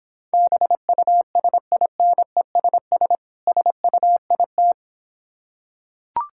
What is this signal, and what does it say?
Morse code, heard as a keyed tone, about 26 wpm, 690 Hz. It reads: BUSINESS SUIT